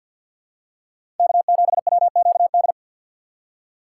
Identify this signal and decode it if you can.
Morse code, heard as a keyed tone, about 54 wpm, 700 Hz. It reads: K7WQD
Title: USA Call Signs - 6x Character Spacing - 50wpm